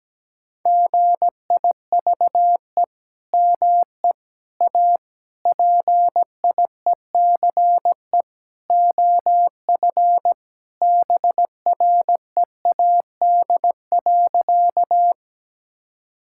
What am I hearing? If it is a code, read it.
Morse code, heard as a keyed tone, about 17 wpm, 700 Hz. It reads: GIVE ME A PIECE OF BREAD.
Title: Sentences from Top 1600 Words 17wpm